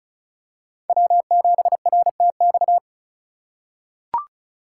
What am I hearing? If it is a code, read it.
Morse code, heard as a keyed tone, about 36 wpm, 700 Hz. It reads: W7RTX
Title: USA Call Signs - 5 Characters - 4x Character Spacing - 35wpm